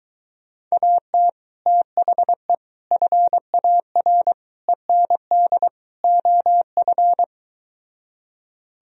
Morse code, heard as a keyed tone, about 24 wpm, 700 Hz. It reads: AT THE FAR END OF